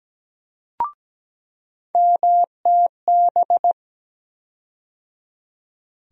Morse code, heard as a keyed tone, about 17 wpm, 700 Hz. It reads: MTB
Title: USA Call Sign Postfixes - 3 Letters 17wpm